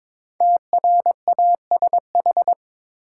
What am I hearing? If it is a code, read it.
Morse code, heard as a keyed tone, about 21 wpm, 700 Hz. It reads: TRASH
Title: Five Letter Words - 22wpm